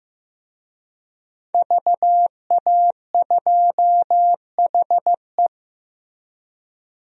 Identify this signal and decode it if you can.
Morse code, heard as a keyed tone, about 15 wpm, 700 Hz. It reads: VA2HE
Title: DXCC-100 Call Signs - 5 Characters - Repeated 2x - 15wpm